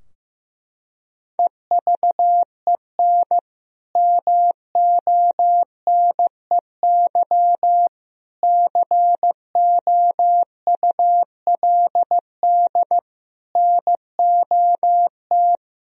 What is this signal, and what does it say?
Morse code, heard as a keyed tone, about 15 wpm, 700 Hz. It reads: EVEN MONEY COULD NOT
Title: Sentences from Top 1000 Words - Review Part 4 - 15wpm